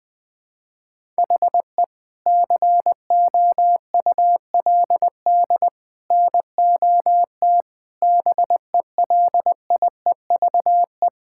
Morse code, heard as a keyed tone, about 21 wpm, 700 Hz. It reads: HE COULD NOT BELIEVE
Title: Sentences with 3-Word Phrases - 20wpm